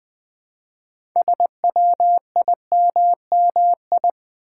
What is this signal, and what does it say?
Morse code, heard as a keyed tone, about 20 wpm, 700 Hz. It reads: SWIMMI
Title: Build A Word - 3 Letters At A Time - 8-Letter Words - 20wpm